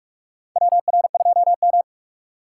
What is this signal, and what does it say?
Morse code, heard as a keyed tone, about 46 wpm, 700 Hz. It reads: WR2M
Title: USA Call Signs - 4 Characters - 2x Character Spacing - 45wpm